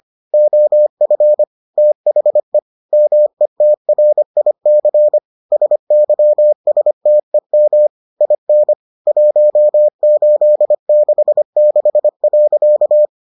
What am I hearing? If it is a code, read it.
Morse code, heard as a keyed tone, about 25 wpm, 600 Hz. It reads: OF THE METRIC SYSTEM IN 1866.